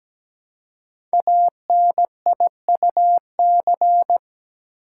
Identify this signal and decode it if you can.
Morse code, heard as a keyed tone, about 17 wpm, 700 Hz. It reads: ANIUC
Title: Code Groups - 5 Letter:Numbers - 17wpm